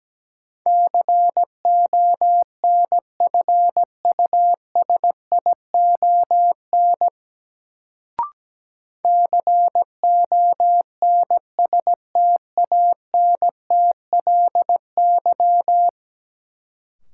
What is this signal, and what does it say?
Morse code, heard as a keyed tone, about 17 wpm, 700 Hz. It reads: CONFUSION CONSTANTLY
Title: Sentences from Top 1600 Words - New Word Drill Alt - 17wpm